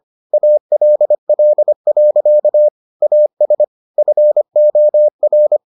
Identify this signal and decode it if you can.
Morse code, heard as a keyed tone, about 25 wpm, 600 Hz. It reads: ALL. AS FOR